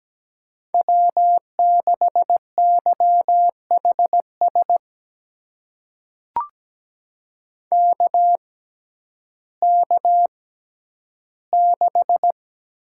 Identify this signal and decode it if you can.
Morse code, heard as a keyed tone, about 17 wpm, spaced as for 10 wpm, 700 Hz. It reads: W6YHS KK6
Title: USA Call Signs - 6x Character Spacing - 17wpm